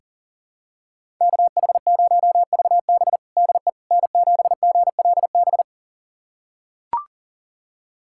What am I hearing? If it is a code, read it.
Morse code, heard as a keyed tone, about 40 wpm, 700 Hz. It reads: KH0VB DE N7GLB
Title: USA Callsign DE Callsign - 5 Characters - 40wpm